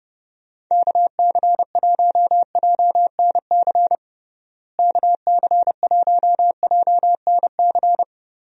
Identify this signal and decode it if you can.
Morse code, heard as a keyed tone, about 30 wpm, 700 Hz. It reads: KC1JNC KC1JNC
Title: USA Call Signs - 6 Characters - Repeated 2x - 30wpm